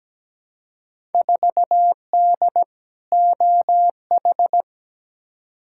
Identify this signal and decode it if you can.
Morse code, heard as a keyed tone, about 17 wpm, 700 Hz. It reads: 4D OH
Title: Field Day Exchanges Encore - 17wpm